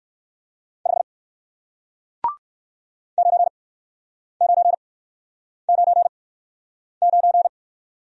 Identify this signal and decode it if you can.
Morse code, heard as a keyed tone, about 54 wpm, spaced as for 9 wpm, 700 Hz. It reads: 5 9999
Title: Speed-Racing Single Letter-Number 45wpm